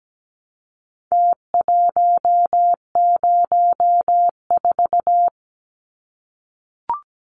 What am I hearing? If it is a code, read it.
Morse code, heard as a keyed tone, about 17 wpm, 700 Hz. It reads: T104
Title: Numbers - Contest Style - 4-Digit 17wpm